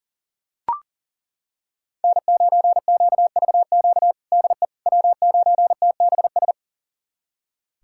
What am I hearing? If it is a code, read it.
Morse code, heard as a keyed tone, about 37 wpm, 700 Hz. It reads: N9QVQ DE W9TBS
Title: USA Callsign DE Callsign - 5 Characters - 40wpm